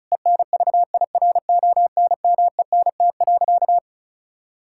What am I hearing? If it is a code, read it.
Morse code, heard as a keyed tone, about 34 wpm, 700 Hz. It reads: ENVIRONMENT.
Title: Sentences from Top 1000 Words - Review Part 6 - 35wpm